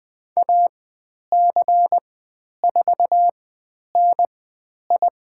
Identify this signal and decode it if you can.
Morse code, heard as a keyed tone, about 20 wpm, spaced as for 10 wpm, 700 Hz. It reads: AC4NI
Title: 095 Lesson 16g - Letter c - Call Signs 20wpm